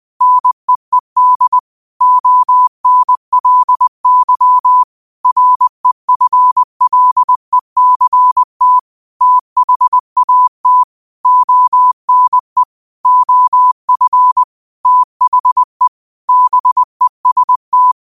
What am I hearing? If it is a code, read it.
Morse code, heard as a keyed tone, about 20 wpm, 1000 Hz. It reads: NEED ONLY REFLECT THAT ONE OF THE BEST